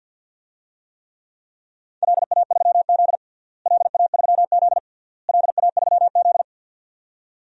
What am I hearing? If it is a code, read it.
Morse code, heard as a keyed tone, about 53 wpm, 700 Hz. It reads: LA3Z LA3Z LA3Z
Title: DXCC-100 Call Signs - 4 Characters - Repeated 3x - 50wpm